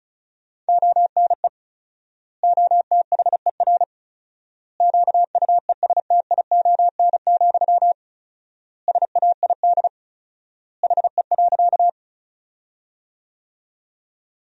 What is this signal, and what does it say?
Morse code, heard as a keyed tone, about 36 wpm, 700 Hz. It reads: ONE OTHER QUESTION, SAID HE.
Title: 4x Word-Spacing - Sentences from Top 400 Words - 35wpm